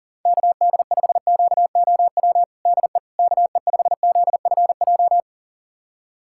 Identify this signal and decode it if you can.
Morse code, heard as a keyed tone, about 40 wpm, 700 Hz. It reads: KD5QOW DE KE5ZFJ
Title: USA Callsign DE Callsign 40wpm